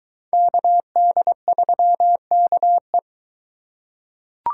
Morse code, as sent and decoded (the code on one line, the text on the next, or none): -.- -.. ...-- -.- .
KD3KE